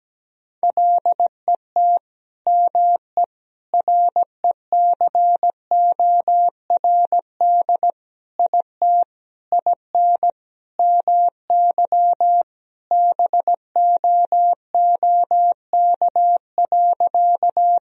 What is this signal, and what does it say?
Morse code, heard as a keyed tone, about 17 wpm, 700 Hz. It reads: LET ME RECORD IT IN MY BOOK.
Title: Sentences from Top 1000 Words - Review Part 3 - 17wpm